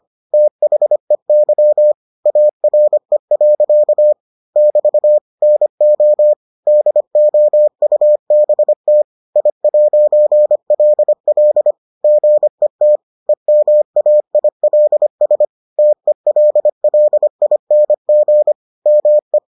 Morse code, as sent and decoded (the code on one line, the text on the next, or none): - .... -.-- / .- .-. .-.-.- / -...- / -. --- / -.. --- ..- -... - / .. .----. .-.. .-.. / --. - / . -- .- .. .-.. ... / - . .-.. .-.. .. -. --. / -- .
THEY ARE. <BT> NO DOUBT I'LL GET EMAILS TELLING ME